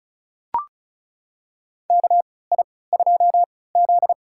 Morse code, heard as a keyed tone, about 34 wpm, spaced as for 20 wpm, 700 Hz. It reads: KI2Z
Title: USA Call Signs - 4 Characters - 3x Character Spacing - 35wpm